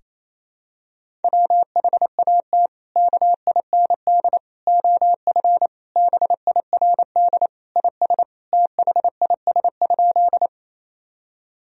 Morse code, heard as a keyed tone, about 28 wpm, 700 Hz. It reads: WHAT KIND OF BIRD IS THIS?
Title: Sentences from Top 1000 Words - Review Part 2 - 28wpm